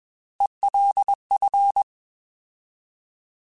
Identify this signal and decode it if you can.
Morse code, heard as a keyed tone, about 21 wpm, 800 Hz. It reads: ELF
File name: Alphabetical / Alphabetical mp3s / Es @20